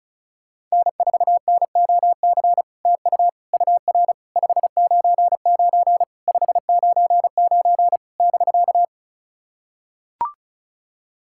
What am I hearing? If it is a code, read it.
Morse code, heard as a keyed tone, about 34 wpm, 660 Hz. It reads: N4NOC TU UR 599 599 <BK>
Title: SOTA QSO Practice 35wpm